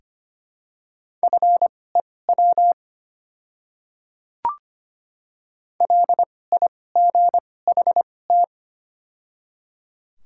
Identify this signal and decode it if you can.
Morse code, heard as a keyed tone, about 25 wpm, spaced as for 18 wpm, 700 Hz. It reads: FEW LIGHT
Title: 2x Character Spacing - Top 300 Words - 25wpm